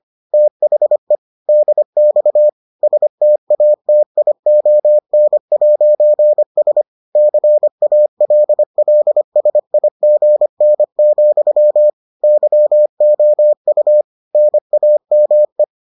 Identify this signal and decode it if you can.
Morse code, heard as a keyed tone, about 25 wpm, 600 Hz. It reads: THE DX STATION'S CALLSIGN, YOU NAME